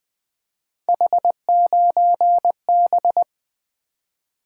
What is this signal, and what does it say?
Morse code, heard as a keyed tone, about 20 wpm, 700 Hz. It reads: H9B
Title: Three Letter-Number Combinations - 20wpm